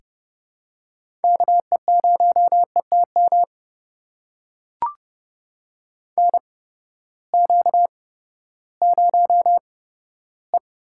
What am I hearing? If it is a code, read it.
Morse code, heard as a keyed tone, about 30 wpm, spaced as for 15 wpm, 700 Hz. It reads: KE0ETM NQ0E
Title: USA Call Signs - 8x Character Spacing - 30wpm